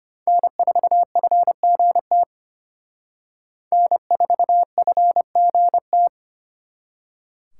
Morse code, heard as a keyed tone, about 32 wpm, 700 Hz. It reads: N4FGT N4FGT